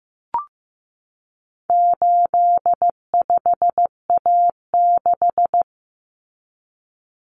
Morse code, heard as a keyed tone, about 15 wpm, 700 Hz. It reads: 85A6
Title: Common Cut Numbers - 4-Digit 15wpm